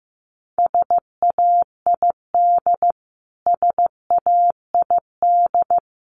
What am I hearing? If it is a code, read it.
Morse code, heard as a keyed tone, about 15 wpm, 700 Hz. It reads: SAID SAID
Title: Top 200 Words Repeated 15wpm